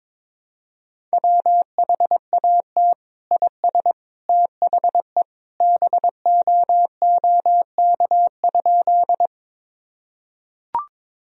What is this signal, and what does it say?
Morse code, heard as a keyed tone, about 22 wpm, 700 Hz. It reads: WHAT IS THE BOOK?
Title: Sentences from Top 1000 Words - Review Part 1 - 22wpm